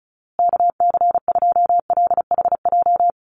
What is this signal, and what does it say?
Morse code, heard as a keyed tone, about 34 wpm, 700 Hz. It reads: KC2LHJ
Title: USA Call Signs Repeated 2x 35wpm